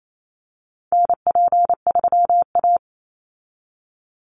Morse code, heard as a keyed tone, about 28 wpm, 700 Hz. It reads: NP3A
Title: Call Signs Repeated 3x - 28wpm